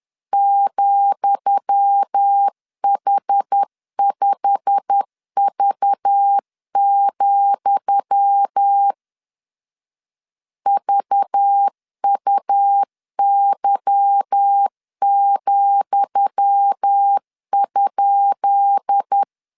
Morse code, heard as a keyed tone, about 10 wpm, 800 Hz. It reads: ,H5V, VUY,?
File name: LEZIONI / 10 SIGNS / 10D COMA 10wpm 3013